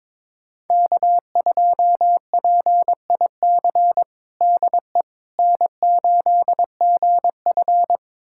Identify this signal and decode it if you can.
Morse code, heard as a keyed tone, about 22 wpm, 700 Hz. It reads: K2PIC DE N8GF
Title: USA Callsign DE Callsign 22wpm